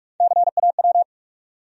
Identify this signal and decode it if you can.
Morse code, heard as a keyed tone, about 42 wpm, 700 Hz. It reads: CAW
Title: USA Call Sign Postfixes - 3 Letters 45wpm